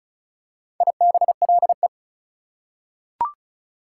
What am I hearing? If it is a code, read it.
Morse code, heard as a keyed tone, about 34 wpm, 700 Hz. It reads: IBLE
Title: Prefixes and Suffixes 35wpm